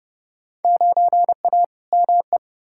Morse code, heard as a keyed tone, about 30 wpm, 700 Hz. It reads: 9A ME
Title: Field Day Exchanges Encore - 30wpm